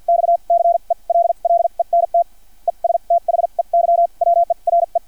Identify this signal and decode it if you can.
Morse code, heard as a keyed tone, about 50 wpm, spaced as for 39 wpm, 680 Hz. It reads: XCELLENT ESTHEY WERE
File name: archived / CRSnet